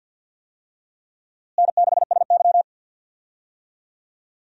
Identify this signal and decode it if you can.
Morse code, heard as a keyed tone, about 49 wpm, 700 Hz. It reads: N6SY
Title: USA Call Signs - 4 Characters - Repeated 3x - 50wpm